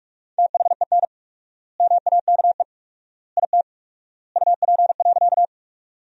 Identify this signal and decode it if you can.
Morse code, heard as a keyed tone, about 47 wpm, 700 Hz. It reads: THEN MAKE IT UP.
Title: 4x Word-Spacing - Sentences from Top 100 Words - 45wpm